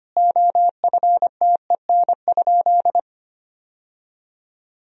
Morse code, heard as a keyed tone, about 25 wpm, 700 Hz. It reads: OFTEN?
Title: 3x Word-Spacing - Sentences from Top 400 Words - 25wpm